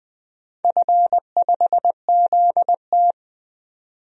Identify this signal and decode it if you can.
Morse code, heard as a keyed tone, about 20 wpm, 700 Hz. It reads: F5ZT
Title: DXCC-100 Call Signs - 4 Characters - 20wpm